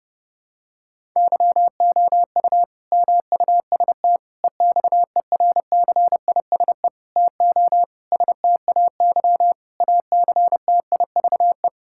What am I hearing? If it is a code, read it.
Morse code, heard as a keyed tone, about 32 wpm, 700 Hz. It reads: YOU MUST EXERCISE TO STAY ACTIVE
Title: Sentences from Top 1000 Words - Review Part 6 - 30wpm